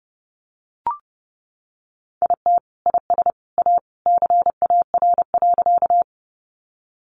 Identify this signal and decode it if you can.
Morse code, heard as a keyed tone, about 30 wpm, 700 Hz. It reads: IT IS A CAR.
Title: Easy Phrases 30wpm